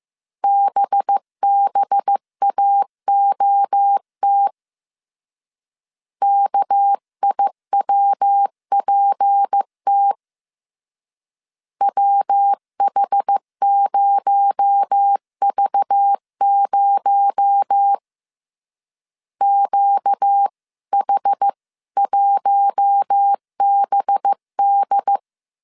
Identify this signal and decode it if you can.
Morse code, heard as a keyed tone, about 15 wpm, 800 Hz. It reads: BBAOT KIWPT WH0V0 QH1BD